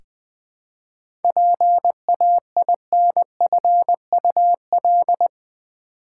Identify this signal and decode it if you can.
Morse code, heard as a keyed tone, about 20 wpm, 700 Hz. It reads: PAINFUL
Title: 7 Letter Words - First 3 Letters Given - 20wpm